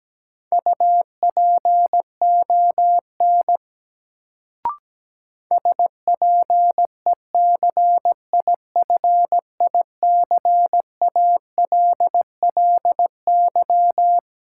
Morse code, heard as a keyed tone, about 17 wpm, 700 Hz. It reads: UPON SPECIFICALLY
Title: Sentences from Top 1100 Words - New Word Drill Alt - 17wpm